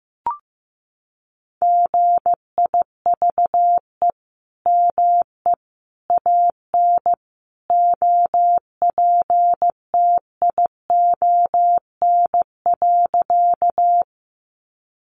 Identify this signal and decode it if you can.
Morse code, heard as a keyed tone, about 15 wpm, 700 Hz. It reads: GIVE ME AN OPTION.
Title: Sentences from Top 1300 Words 15wpm